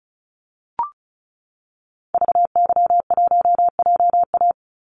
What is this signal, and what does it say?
Morse code, heard as a keyed tone, about 34 wpm, 700 Hz. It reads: VY1JA
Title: Call Signs Repeated 3x - 35wpm